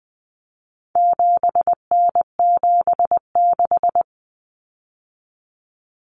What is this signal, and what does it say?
Morse code, heard as a keyed tone, about 20 wpm, 700 Hz. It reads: 7N76
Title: Common Cut Numbers - 4-Digit 20wpm